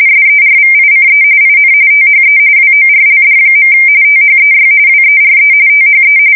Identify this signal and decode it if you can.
RTTY, about 45.45 baud. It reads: SERVICE IN THE WORLD. THEY WERE THE
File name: fsk170 / USS-New-Jersey